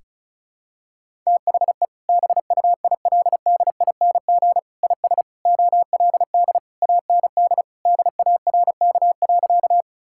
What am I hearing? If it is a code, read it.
Morse code, heard as a keyed tone, about 36 wpm, 700 Hz. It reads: THE BUILDING IS OLD AND DARK.
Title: Sentences from Top 1000 Words - Review Part 5 - 35wpm